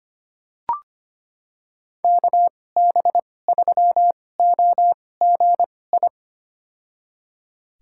Morse code, heard as a keyed tone, about 25 wpm, spaced as for 18 wpm, 700 Hz. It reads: KB3OGI
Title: USA Call Signs - 6 Characters - 2x Character Spacing - 25wpm